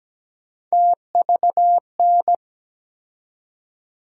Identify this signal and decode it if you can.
Morse code, heard as a keyed tone, about 17 wpm, 700 Hz. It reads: TVN
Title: USA Call Sign Postfixes - 3 Letters 17wpm